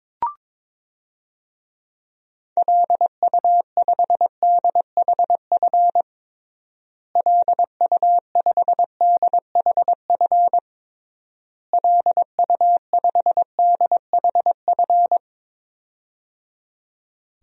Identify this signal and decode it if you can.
Morse code, heard as a keyed tone, about 22 wpm, 700 Hz. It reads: LU5DHF LU5DHF LU5DHF
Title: DXCC-100 Call Signs - 6 Characters - Repeated 3x - 22wpm